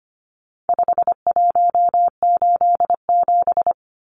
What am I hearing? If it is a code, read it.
Morse code, heard as a keyed tone, about 25 wpm, 700 Hz. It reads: 5187
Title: Numbers - 4-Digit 25wpm